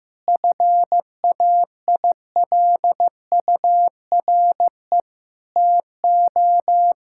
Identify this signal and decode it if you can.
Morse code, heard as a keyed tone, about 15 wpm, 700 Hz. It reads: FAILURE TO
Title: Sentences from Top 1000 Words - Review Part 6 - 15wpm